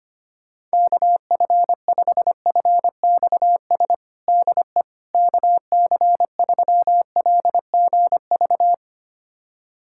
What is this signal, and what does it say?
Morse code, heard as a keyed tone, about 25 wpm, 700 Hz. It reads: KF5FXS DE KC3LGV
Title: USA Callsign DE Callsign - 6 Characters - 25wpm